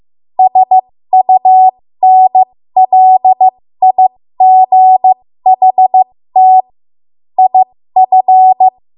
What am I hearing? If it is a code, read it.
Morse code, heard as a keyed tone, about 15 wpm, spaced as for 13 wpm, 750 Hz. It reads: SUNLIGHT IF
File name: Morse / 13WPM